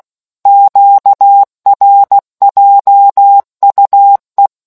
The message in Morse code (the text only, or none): QRJUE